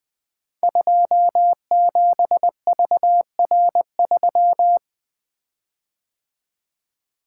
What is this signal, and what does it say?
Morse code, heard as a keyed tone, about 20 wpm, 700 Hz. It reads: 27VR3
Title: Code Groups - 5 Letter:Numbers - 20wpm